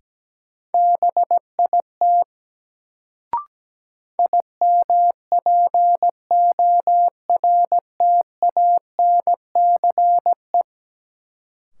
Morse code, heard as a keyed tone, about 17 wpm, 700 Hz. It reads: BIT IMPORTANCE
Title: Sets of 3 Words - New Word Drill - 17wpm